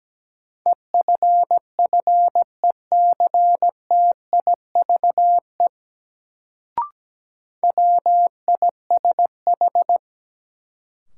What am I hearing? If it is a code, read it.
Morse code, heard as a keyed tone, about 17 wpm, 700 Hz. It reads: EFFECTIVE WISH
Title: Sentences from Top 1600 Words - New Word Drill Alt - 17wpm